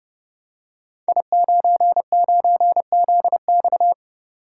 Morse code, heard as a keyed tone, about 28 wpm, 700 Hz. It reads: I99ZX
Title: Code Groups - 5 Letter:Numbers - 30wpm